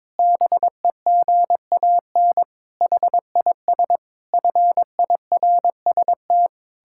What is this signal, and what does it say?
Morse code, heard as a keyed tone, about 22 wpm, 700 Hz. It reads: BEGAN HIS FIRST
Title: Pangram Challenge v2 - 22wpm